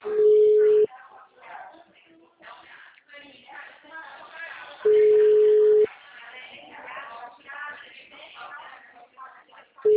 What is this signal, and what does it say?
Ringback tone (Europe, 425 Hz), with speech.